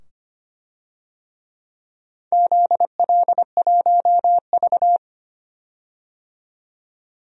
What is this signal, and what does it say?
Morse code, heard as a keyed tone, about 25 wpm, 700 Hz. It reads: ZL1V
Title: DXCC-100 Call Signs - 4 Characters - 25wpm